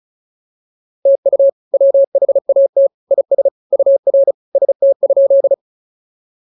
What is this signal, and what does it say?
Morse code, heard as a keyed tone, about 36 wpm, 590 Hz. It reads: TU WHAT IS UR ST?